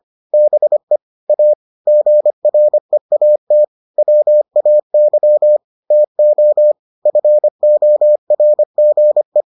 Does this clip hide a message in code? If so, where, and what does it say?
Yes: Morse code from 0.3 to 9.4 s, heard as a keyed tone, about 25 wpm, 600 Hz. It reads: BE A GREAT WAY TO FORGE